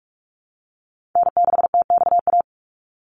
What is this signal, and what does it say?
Morse code, heard as a keyed tone, about 44 wpm, 700 Hz. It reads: N6TXA
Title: USA Call Signs 45wpm